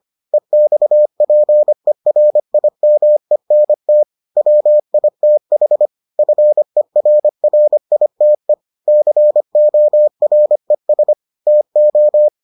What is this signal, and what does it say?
Morse code, heard as a keyed tone, about 25 wpm, 600 Hz. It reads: EXPERIMENT WITH FERRITE CORES TO